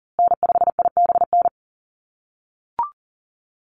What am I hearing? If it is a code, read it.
Morse code, heard as a keyed tone, about 40 wpm, 700 Hz. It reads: N5IBN
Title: USA Call Signs Repeated 2x 40wpm